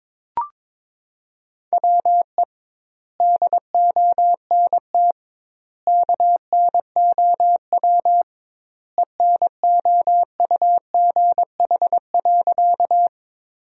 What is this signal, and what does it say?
Morse code, heard as a keyed tone, about 22 wpm, 700 Hz. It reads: WE DONT KNOW ENOUGH.
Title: 2x Word-Spacing - Sentences from Top 400 Words - 22wpm